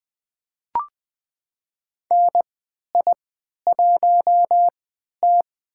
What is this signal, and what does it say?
Morse code, heard as a keyed tone, about 20 wpm, spaced as for 11 wpm, 700 Hz. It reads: NI1T
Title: USA Call Signs - 4 Characters - 3x Character Spacing - 20wpm